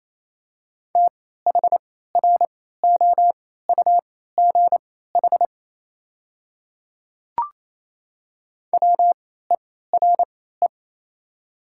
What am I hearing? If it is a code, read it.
Morse code, heard as a keyed tone, about 28 wpm, spaced as for 16 wpm, 700 Hz. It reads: THROUGH WERE
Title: 3x Character Spacing - Top 200 Words - 28wpm